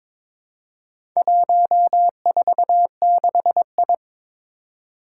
Morse code, heard as a keyed tone, about 21 wpm, 700 Hz. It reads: 146I
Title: Four Challenging Letter-Number Combinations 22wpm